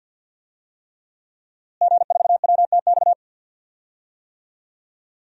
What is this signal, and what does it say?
Morse code, heard as a keyed tone, about 47 wpm, 700 Hz. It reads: G4WTX
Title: DXCC-100 Call Signs - 5 Characters - Repeated 3x - 50wpm